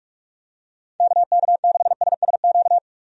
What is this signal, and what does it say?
Morse code, heard as a keyed tone, about 45 wpm, 700 Hz. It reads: KK6SSQ
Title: USA Call Signs - 6 Characters - 45wpm